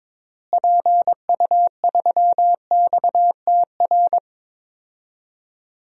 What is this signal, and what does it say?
Morse code, heard as a keyed tone, about 21 wpm, 700 Hz. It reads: PU3XTR
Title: DXCC-100 Call Signs - 6 Characters - Repeated 2x - 22wpm